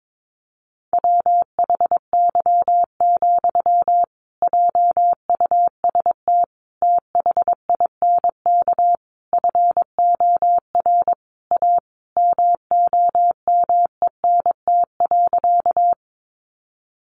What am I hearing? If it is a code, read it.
Morse code, heard as a keyed tone, about 22 wpm, 700 Hz. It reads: WHY, JUST THINK FOR A MOMENT.
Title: Sentences from Top 1000 Words 22wpm